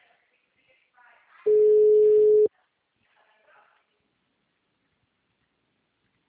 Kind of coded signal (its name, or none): telephone call progress tone